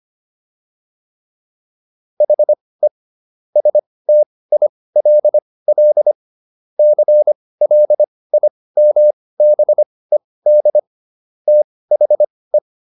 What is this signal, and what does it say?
Morse code, heard as a keyed tone, about 25 wpm, spaced as for 18 wpm, 600 Hz. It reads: HE STILL CLIMBED THE